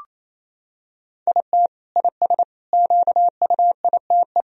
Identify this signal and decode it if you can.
Morse code, heard as a keyed tone, about 28 wpm, 700 Hz. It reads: IT IS QUITE
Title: Sentences from Top 1000 Words - Review Part 4 - 28wpm